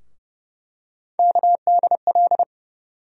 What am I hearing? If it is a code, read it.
Morse code, heard as a keyed tone, about 30 wpm, 700 Hz. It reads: KDL
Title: USA Call Sign Postfixes - 3 Letters 30wpm